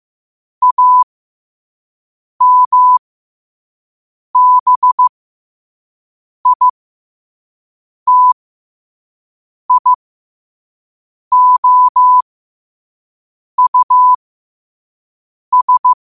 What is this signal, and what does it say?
Morse code, heard as a keyed tone, about 15 wpm, spaced as for 5 wpm, 1000 Hz. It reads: AMBITIOUS
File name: qotdfiles / 5wpm